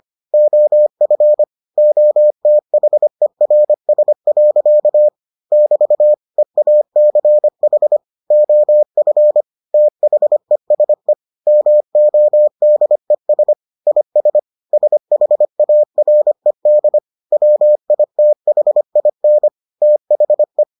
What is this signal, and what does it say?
Morse code, heard as a keyed tone, about 25 wpm, 600 Hz. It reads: OF OTHERS. <BT> EACH OF THESE MODES IS SHARED WITHIN THE